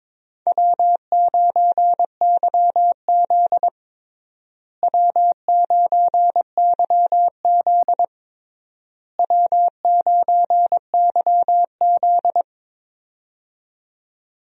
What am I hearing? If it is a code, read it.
Morse code, heard as a keyed tone, about 22 wpm, 700 Hz. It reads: W9YZ W9YZ W9YZ
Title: USA Call Signs - 4 Characters - Repeated 3x - 22wpm